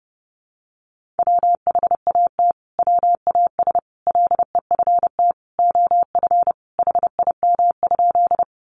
Morse code, heard as a keyed tone, about 28 wpm, 700 Hz. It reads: WHAT WAS LEFT OF HIM?
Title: Sentences from Top 300 Words 30wpm